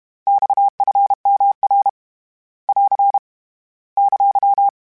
Morse code, heard as a keyed tone, about 32 wpm, 800 Hz. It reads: XFMR <AR> !
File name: Abreviations / Abreviations mp3s / Abreviations@30